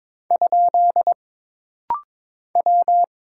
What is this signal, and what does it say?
Morse code, heard as a keyed tone, about 22 wpm, 700 Hz. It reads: ? W
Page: This rapid-fire timing is geared to push your ICR (Instant Character Recognition) skills.